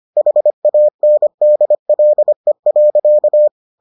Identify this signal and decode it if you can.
Morse code, heard as a keyed tone, about 25 wpm, 600 Hz. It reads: HANDLE.